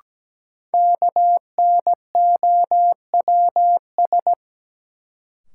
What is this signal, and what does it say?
Morse code, heard as a keyed tone, about 17 wpm, 700 Hz. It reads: KNOWS
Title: Sentences from Top 1000 Words - Review Part 1 - New Word Drill - 17wpm